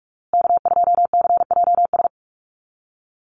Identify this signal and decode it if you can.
Morse code, heard as a keyed tone, about 44 wpm, 690 Hz. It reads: K2CJS
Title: USA Call Signs Repeated 3x 45wpm